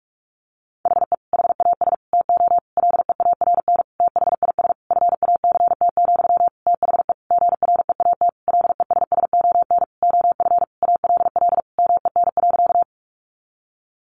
Morse code, heard as a keyed tone, about 44 wpm, 700 Hz. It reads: HE HAS TO LEARN THIS FACT, THE GREAT LESSON OF ALL MEN.
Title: Sentences from Top 1200 Words 45wpm